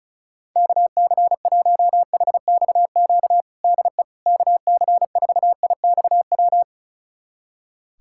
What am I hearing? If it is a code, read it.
Morse code, heard as a keyed tone, about 36 wpm, 700 Hz. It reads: KC1HXQ DE KC4IXW
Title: USA Callsign DE Callsign - 6 Characters - 35wpm